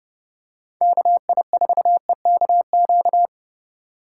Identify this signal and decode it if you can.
Morse code, heard as a keyed tone, about 30 wpm, 700 Hz. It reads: KI4EKQ
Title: USA Call Signs - 6 Characters - 4x Character Spacing - 30wpm